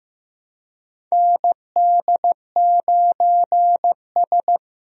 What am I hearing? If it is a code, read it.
Morse code, heard as a keyed tone, about 15 wpm, 700 Hz. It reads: ND9S
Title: USA Call Signs - 4 Characters - Repeated 3x - 15wpm